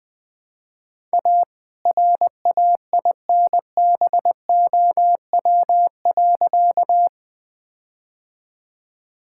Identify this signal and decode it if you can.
Morse code, heard as a keyed tone, about 21 wpm, 700 Hz. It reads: A RAINBOW.
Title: All-Age Appropriate Jokes - 20wpm